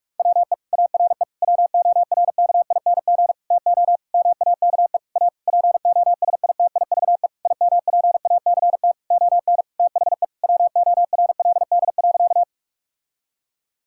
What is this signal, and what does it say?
Morse code, heard as a keyed tone, about 47 wpm, 700 Hz. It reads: WE ARE WORKING TO MAKE A POSITIVE IMPACT ON THE WORLD.
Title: Sentences from Top 1000 Words - Review Part 6 - 45wpm